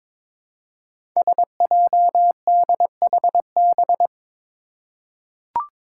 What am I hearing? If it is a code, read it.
Morse code, heard as a keyed tone, about 22 wpm, 700 Hz. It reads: SJDHB
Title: Five Challenging Letter-Number Combinations 22wpm